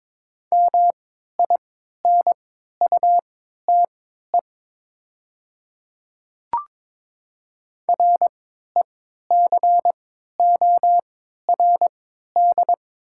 Morse code, heard as a keyed tone, about 22 wpm, spaced as for 12 wpm, 700 Hz. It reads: MINUTE RECORD
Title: 3x Character Spacing - Top 500 Words - 22wpm